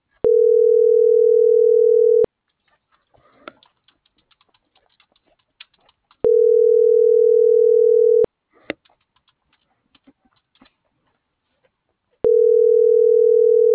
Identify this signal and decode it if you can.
Ringback tone (North America), 440 + 480 Hz.